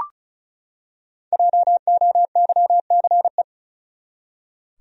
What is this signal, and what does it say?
Morse code, heard as a keyed tone, about 34 wpm, 700 Hz. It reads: JOYCE
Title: USA Common Names - Max 5 Letters - 35wpm